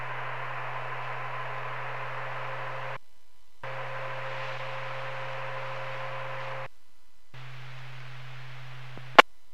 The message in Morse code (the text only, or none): none